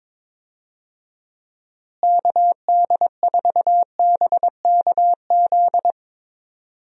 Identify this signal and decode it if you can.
Morse code, heard as a keyed tone, about 22 wpm, 700 Hz. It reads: KD4BKZ
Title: USA Call Signs - 6 Characters - Repeated 3x - 22wpm